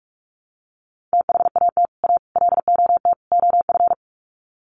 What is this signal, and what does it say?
Morse code, heard as a keyed tone, about 44 wpm, 700 Hz. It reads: THAT A LOT OF
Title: Sets of 4 Words 45wpm